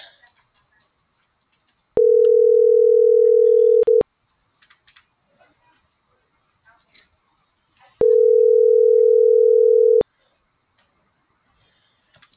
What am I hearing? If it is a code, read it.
Ringback tone (North America), 440 + 480 Hz.